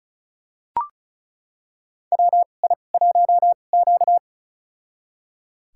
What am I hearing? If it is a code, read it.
Morse code, heard as a keyed tone, about 36 wpm, spaced as for 25 wpm, 700 Hz. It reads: WI1Q
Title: USA Call Signs - 4 Characters - 2x Character Spacing - 35wpm